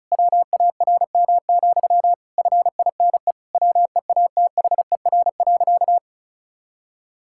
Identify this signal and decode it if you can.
Morse code, heard as a keyed tone, about 34 wpm, 690 Hz. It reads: WARM, FINE WEATHER.
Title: Sentences from Top 1000 Words - Review Part 5 - 35wpm